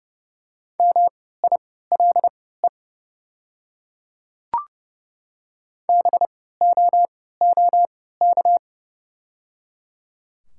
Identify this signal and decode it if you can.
Morse code, heard as a keyed tone, about 28 wpm, spaced as for 17 wpm, 690 Hz. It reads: MILE BOOK
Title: 3x Character Spacing - Top 500 Words - 30wpm